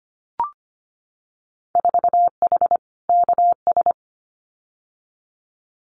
Morse code, heard as a keyed tone, about 25 wpm, 700 Hz. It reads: 4H KS
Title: Winter Field Day Exchanges Encore 25wpm